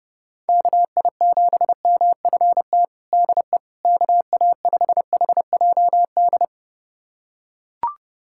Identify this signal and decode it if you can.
Morse code, heard as a keyed tone, about 29 wpm, 700 Hz. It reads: KI7MFT DE KA5HJD
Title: USA Callsign DE Callsign - 6 Characters - 30wpm